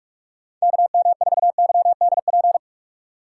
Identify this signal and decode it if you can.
Morse code, heard as a keyed tone, about 43 wpm, 690 Hz. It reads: KM4YDP
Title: USA Call Signs - 6 Characters - 3x Character Spacing - 45wpm